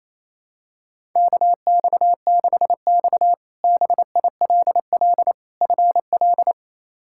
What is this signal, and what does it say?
Morse code, heard as a keyed tone, about 28 wpm, 700 Hz. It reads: KX6X BILL FL
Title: K1USN SST QSO Practice - 28wpm